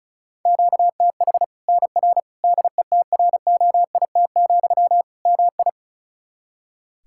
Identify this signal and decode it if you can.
Morse code, heard as a keyed tone, about 34 wpm, 700 Hz. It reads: QTH NR DETROIT, MI
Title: K1USN SST QSO Practice 35wpm